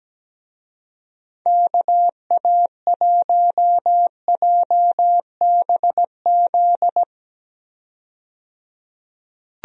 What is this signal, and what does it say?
Morse code, heard as a keyed tone, about 17 wpm, 700 Hz. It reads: KA1JBZ
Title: USA Call Signs - 6 Characters - 17wpm